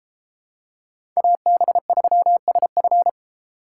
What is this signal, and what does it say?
Morse code, heard as a keyed tone, about 32 wpm, 700 Hz. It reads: AB3SF